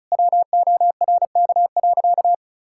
Morse code, heard as a keyed tone, about 34 wpm, 700 Hz. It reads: WORK.
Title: Sentences from Top 1000 Words - Review Part 6 - 35wpm